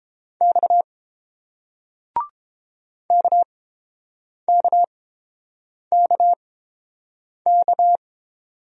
Morse code, heard as a keyed tone, about 30 wpm, spaced as for 7 wpm, 700 Hz. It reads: X KKKK